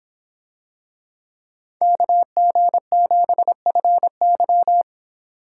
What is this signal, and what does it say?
Morse code, heard as a keyed tone, about 26 wpm, 700 Hz. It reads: KG7FY